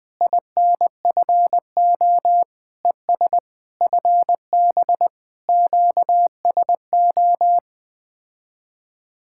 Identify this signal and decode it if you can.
Morse code, heard as a keyed tone, about 20 wpm, 700 Hz. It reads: INFO ES FB QSO
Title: Speed-Racing Common QSO Phrases - 17wpm